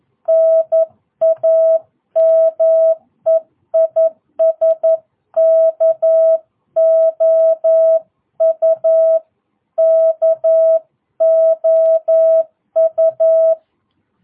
Morse code, heard as a keyed tone, about 11 wpm, 650 Hz. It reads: NAME IS KOU KOU
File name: jh8cbh / 2022cw